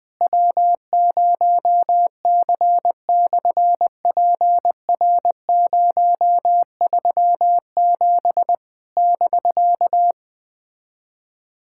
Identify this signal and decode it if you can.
Morse code, heard as a keyed tone, about 20 wpm, 700 Hz. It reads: W0C/PR037 <BK>
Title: SOTA QSO Practice 20wpm